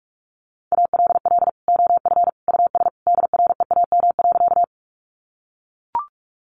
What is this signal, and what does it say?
Morse code, heard as a keyed tone, about 44 wpm, 700 Hz. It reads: ALL OF US DREAM.
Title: Sentences from Top 1500 Words 45wpm